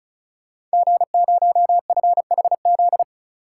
Morse code, heard as a keyed tone, about 36 wpm, 700 Hz. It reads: G0FHZ
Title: Code Groups - 5 Letter:Numbers - 35wpm